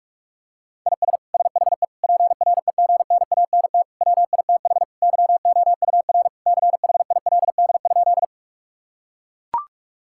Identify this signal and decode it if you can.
Morse code, heard as a keyed tone, about 41 wpm, 700 Hz. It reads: IS SHE PREGNANT WITH YOUR CHILD?
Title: Sentences from Top 1000 Words - Review Part 6 - 45wpm